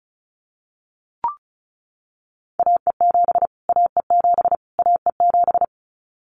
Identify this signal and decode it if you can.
Morse code, heard as a keyed tone, about 34 wpm, 700 Hz. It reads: AE7 AE7 AE7
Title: Callsign Prefixes for Top 25 DXCC - Repeated 3x - 35wpm